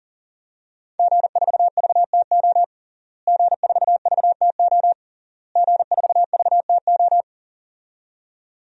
Morse code, heard as a keyed tone, about 37 wpm, 700 Hz. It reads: G4VTO G4VTO G4VTO
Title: DXCC-100 Call Signs - 5 Characters - Repeated 3x - 40wpm